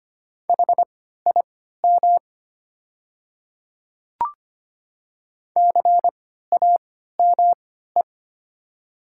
Morse code, heard as a keyed tone, about 25 wpm, spaced as for 14 wpm, 700 Hz. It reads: HIM CAME
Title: 3x Character Spacing - Top 300 Words - 25wpm